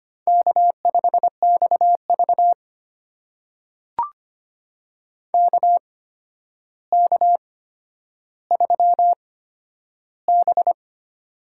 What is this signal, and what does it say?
Morse code, heard as a keyed tone, about 25 wpm, spaced as for 11 wpm, 700 Hz. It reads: K5XV KK3B
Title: USA Call Signs - 4 Characters - 8x Character Spacing - 25wpm